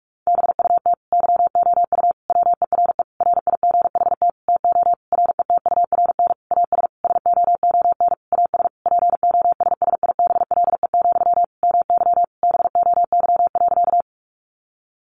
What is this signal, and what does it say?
Morse code, heard as a keyed tone, about 44 wpm, 700 Hz. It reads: BUT YOU WERE RIGHT TO RETURN AS SOON AS POSSIBLE, MY BOY.
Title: Sentences from Top 800 Words 45wpm